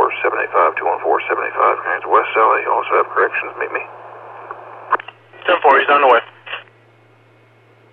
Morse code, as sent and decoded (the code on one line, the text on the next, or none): none